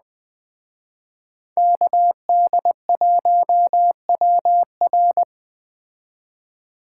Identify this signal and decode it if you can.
Morse code, heard as a keyed tone, about 20 wpm, 700 Hz. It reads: KD1WR